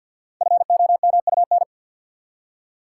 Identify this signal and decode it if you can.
Morse code, heard as a keyed tone, about 49 wpm, 700 Hz. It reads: FOMUN
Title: Code Groups - 5 Letters - 50wpm